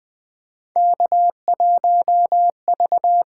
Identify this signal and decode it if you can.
Morse code, heard as a keyed tone, about 20 wpm, 700 Hz. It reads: K1V
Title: USA Call Signs - 4 Characters - 8x Character Spacing - 20wpm